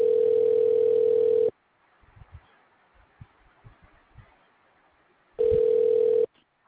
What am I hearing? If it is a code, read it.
Ringback tone (North America), 440 + 480 Hz.
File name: recordings / RE825d b9e8a0f69dbab